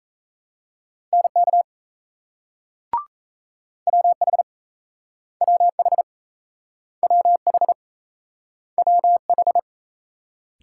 Morse code, heard as a keyed tone, about 37 wpm, 700 Hz. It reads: NK WH WH WH WH